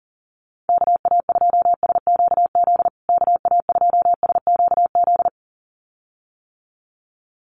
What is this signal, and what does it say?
Morse code, heard as a keyed tone, about 40 wpm, 700 Hz. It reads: KA2SQZ KA2SQZ
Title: USA Call Signs Repeated 2x 40wpm